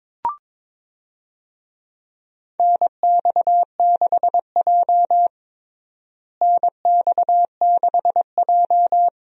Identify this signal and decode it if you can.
Morse code, heard as a keyed tone, about 22 wpm, 700 Hz. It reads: NX6J NX6J
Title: USA Call Signs - 4 Characters - Repeated 3x - 22wpm